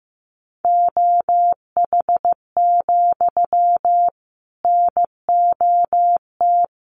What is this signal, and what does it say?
Morse code, heard as a keyed tone, about 15 wpm, 700 Hz. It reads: OH, NOT
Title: Sentences from Top 1200 Words 15wpm